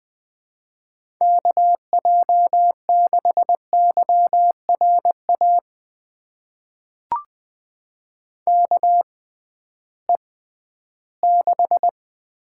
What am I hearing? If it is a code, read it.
Morse code, heard as a keyed tone, about 20 wpm, spaced as for 13 wpm, 700 Hz. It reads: KJ6YRA KE6
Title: USA Call Signs - 6 Characters - 6x Character Spacing - 20wpm